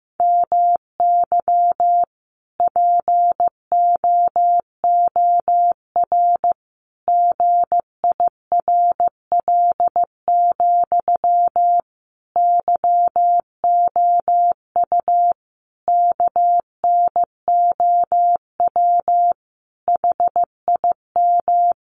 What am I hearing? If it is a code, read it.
Morse code, heard as a keyed tone, about 15 wpm, 700 Hz. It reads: MY POOR GIRL, YOU KNOW HIM
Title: Sentences from Top 1000 Words 15wpm